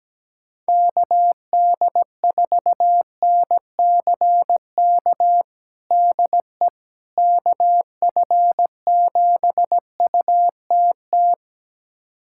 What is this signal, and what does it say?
Morse code, heard as a keyed tone, about 17 wpm, 700 Hz. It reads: KD4NCK DE KF7UTT
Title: USA Callsign DE Callsign 17wpm